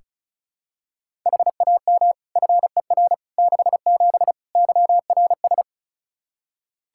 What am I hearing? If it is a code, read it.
Morse code, heard as a keyed tone, about 36 wpm, 700 Hz. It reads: HAM FER 67 YRS
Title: Practice QSO - Exchange 3 - More Info - 35wpm